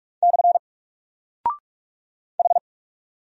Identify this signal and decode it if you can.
Morse code, heard as a keyed tone, about 44 wpm, 700 Hz. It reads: / H
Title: Single Character (Letter:Number:Punctuation:Prosign) 45wpm